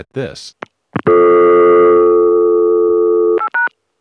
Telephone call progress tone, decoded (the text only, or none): dial tone